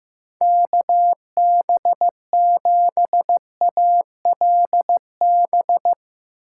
Morse code, heard as a keyed tone, about 15 wpm, 700 Hz. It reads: KB7ALB
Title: USA Call Signs - 6 Characters - 8x Character Spacing - 15wpm